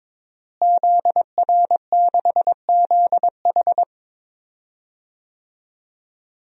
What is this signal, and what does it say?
Morse code, heard as a keyed tone, about 22 wpm, 700 Hz. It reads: ZR6ZH
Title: DXCC-100 Call Signs - 5 Characters - Repeated 3x - 22wpm